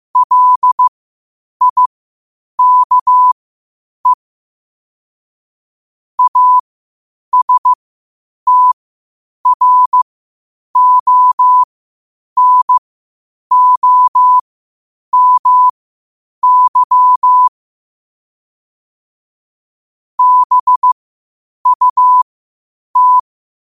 Morse code, heard as a keyed tone, about 15 wpm, spaced as for 8 wpm, 1000 Hz. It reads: LIKE ASTRONOMY BUT